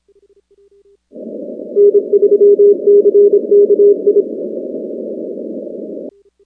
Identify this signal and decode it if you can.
Morse code, heard as a keyed tone, about 27 wpm, 410 Hz. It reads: N3CKI
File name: contests / 2019-cw-wpx-kt5j-k5pi- -2330 0-19